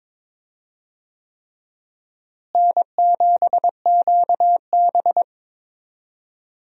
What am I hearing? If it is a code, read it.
Morse code, heard as a keyed tone, about 22 wpm, 700 Hz. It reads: N7QB
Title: USA Call Signs - 4 Characters - Repeated 3x - 22wpm